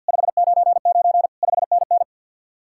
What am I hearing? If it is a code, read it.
Morse code, heard as a keyed tone, about 63 wpm, 700 Hz. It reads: 599 5NN